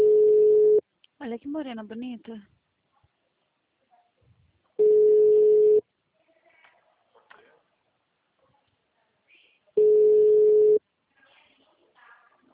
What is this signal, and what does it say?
Ringback tone (Europe, 425 Hz), with speech.